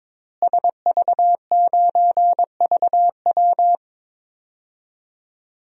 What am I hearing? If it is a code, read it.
Morse code, heard as a keyed tone, about 22 wpm, 700 Hz. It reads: SV9VW
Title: DXCC-100 Call Signs - 5 Characters - Repeated 3x - 22wpm